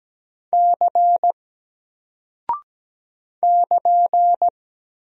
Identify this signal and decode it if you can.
Morse code, heard as a keyed tone, about 17 wpm, 700 Hz. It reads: C <KN>
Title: Single Character (Letter:Number:Punctuation:Prosign) 17wpm